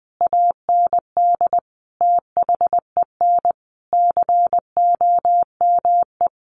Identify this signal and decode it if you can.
Morse code, heard as a keyed tone, about 20 wpm, 700 Hz. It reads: AND THEN COME
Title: Sentences from Top 600 Words 20wpm